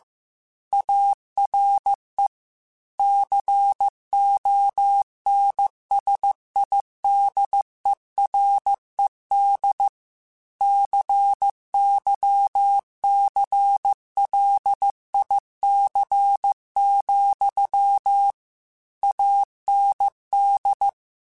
Morse code, heard as a keyed tone, about 15 wpm, 800 Hz. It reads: ARE CONSIDERED CYCLIC, AND
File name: Technical Subjects / Technical Subjects-mp3s / XL @13